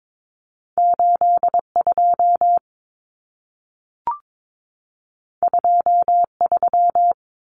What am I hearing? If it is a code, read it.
Morse code, heard as a keyed tone, about 22 wpm, 700 Hz. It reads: 82 23
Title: Numbers - 2-Digit 22wpm